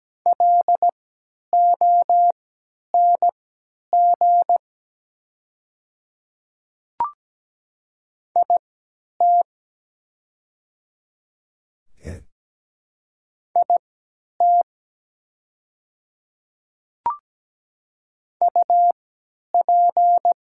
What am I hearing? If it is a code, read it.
Morse code, heard as a keyed tone, about 17 wpm, spaced as for 10 wpm, 700 Hz. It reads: LONG IT IT UP
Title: 3x Character Spacing - Top 100 Words - 17wpm